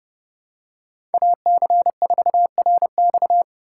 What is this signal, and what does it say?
Morse code, heard as a keyed tone, about 31 wpm, 700 Hz. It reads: AC4RX
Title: USA Call Signs - 5 Characters - 3x Character Spacing - 30wpm